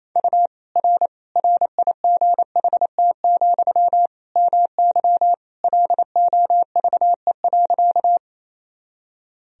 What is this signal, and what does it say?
Morse code, heard as a keyed tone, about 28 wpm, 700 Hz. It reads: U R RIGHT, MY LOVE.
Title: Sentences from Top 400 Words and Abbreviations 28wpm